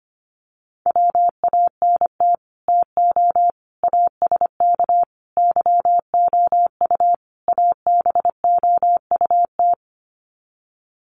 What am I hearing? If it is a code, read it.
Morse code, heard as a keyed tone, about 25 wpm, 700 Hz. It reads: WANT TO ASK YOU ABOUT
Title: Sets of 5 Words Encore2 25wpm